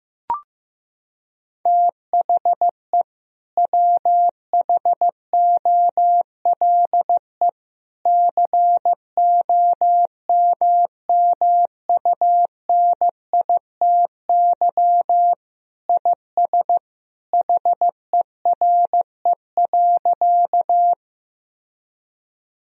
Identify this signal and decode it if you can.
Morse code, heard as a keyed tone, about 15 wpm, 700 Hz. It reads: THE WHOLE COMMUNITY IS HERE.
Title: Sentences from Top 1000 Words - Review Part 4 - 15wpm